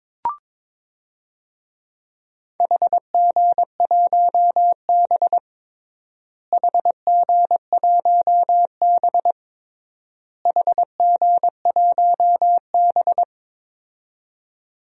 Morse code, heard as a keyed tone, about 22 wpm, 700 Hz. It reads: HG1B HG1B HG1B
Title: DXCC-100 Call Signs - 4 Characters - Repeated 3x - 22wpm